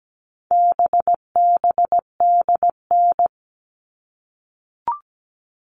Morse code, heard as a keyed tone, about 17 wpm, 700 Hz. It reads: BBDN
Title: All Cut Numbers - 4-Digit 17wpm